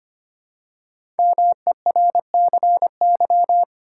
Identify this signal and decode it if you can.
Morse code, heard as a keyed tone, about 25 wpm, 700 Hz. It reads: MERCY
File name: 5 Letter Words - First 3 Letters Given - 25wpm